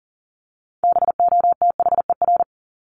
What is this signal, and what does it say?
Morse code, heard as a keyed tone, about 40 wpm, 700 Hz. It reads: BOTHER
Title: Top 1501-1750 Words 40wpm